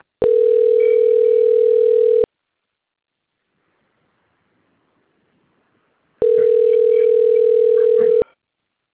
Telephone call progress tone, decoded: ringback tone

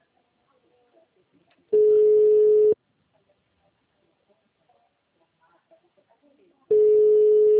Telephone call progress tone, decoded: ringback tone

